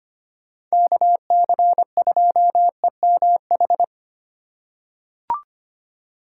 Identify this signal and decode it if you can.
Morse code, heard as a keyed tone, about 25 wpm, 700 Hz. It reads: KC2EMH
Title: USA Call Signs - 6 Characters - 2x Character Spacing - 25wpm